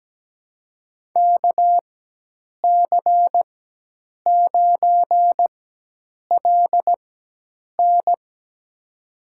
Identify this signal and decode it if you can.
Morse code, heard as a keyed tone, about 17 wpm, spaced as for 8 wpm, 700 Hz. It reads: KC9LN
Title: USA Call Signs - 5 Characters - 4x Character Spacing - 17wpm